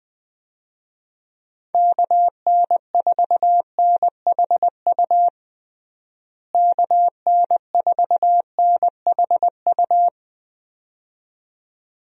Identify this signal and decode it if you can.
Morse code, heard as a keyed tone, about 21 wpm, 700 Hz. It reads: KN4NHU KN4NHU
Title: USA Call Signs - 6 Characters - Repeated 2x - 20wpm